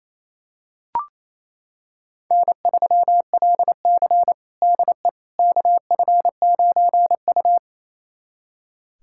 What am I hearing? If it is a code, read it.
Morse code, heard as a keyed tone, about 28 wpm, 700 Hz. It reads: N3LC DE KF9U
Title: USA Callsign DE Callsign - 4 Characters - 28wpm